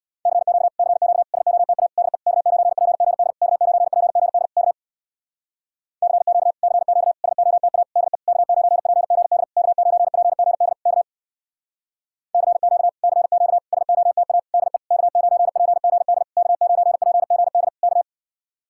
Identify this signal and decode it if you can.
Morse code, heard as a keyed tone, about 67 wpm, 700 Hz. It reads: CQ CQ SOTA DE K9PGD K9PGD K CQ CQ SOTA DE K9PGD K9PGD K CQ CQ SOTA DE K9PGD K9PGD K